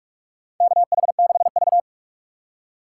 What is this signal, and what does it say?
Morse code, heard as a keyed tone, about 46 wpm, 700 Hz. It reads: KH6V
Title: USA Call Signs - 3x Character Spacing - 45wpm